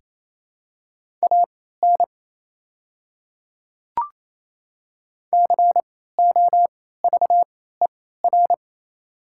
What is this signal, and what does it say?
Morse code, heard as a keyed tone, about 28 wpm, spaced as for 16 wpm, 700 Hz. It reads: AN COVER
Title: 3x Character Spacing - Top 300 Words - 28wpm